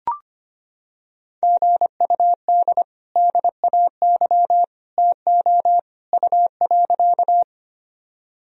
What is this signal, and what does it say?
Morse code, heard as a keyed tone, about 25 wpm, 700 Hz. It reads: GUD DAY TO U.
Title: Sentences from Top 200 Words and Abbreviations 25wpm